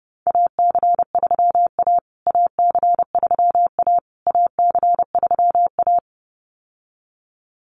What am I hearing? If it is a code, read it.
Morse code, heard as a keyed tone, about 30 wpm, 700 Hz. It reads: AC3A AC3A AC3A
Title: USA Call Signs Repeated 3x 30wpm